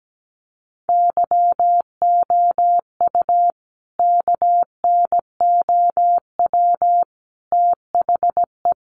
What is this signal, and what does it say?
Morse code, heard as a keyed tone, about 17 wpm, 700 Hz. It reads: YOU KNOW THE
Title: Sentences from Top 1000 Words 17wpm